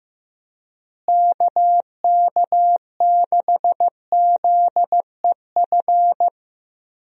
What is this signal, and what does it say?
Morse code, heard as a keyed tone, about 15 wpm, 700 Hz. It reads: KK6ZEF
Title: USA Call Signs - 6 Characters - Repeated 3x - 15wpm